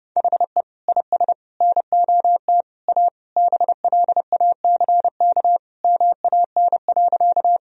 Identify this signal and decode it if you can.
Morse code, heard as a keyed tone, about 30 wpm, 700 Hz. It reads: HE IS NOT A BLACK MAN.
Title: Sentences from Top 1000 Words - Review Part 2 - 30wpm